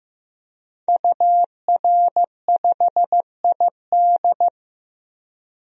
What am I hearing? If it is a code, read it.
Morse code, heard as a keyed tone, about 15 wpm, 700 Hz. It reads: UR5ID
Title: DXCC-100 Call Signs - 5 Characters - Repeated 3x - 15wpm